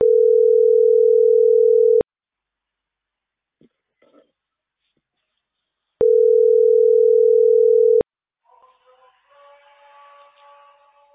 Ringback tone (North America), 440 + 480 Hz.